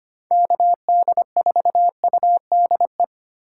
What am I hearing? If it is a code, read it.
Morse code, heard as a keyed tone, about 25 wpm, 700 Hz. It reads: KD4UDE
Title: USA Call Signs - 3x Character Spacing - 25wpm